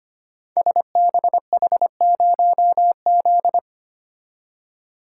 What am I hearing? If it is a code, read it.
Morse code, heard as a keyed tone, about 25 wpm, 700 Hz. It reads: SBH0Z
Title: Code Groups - 5 Letter:Numbers - 25wpm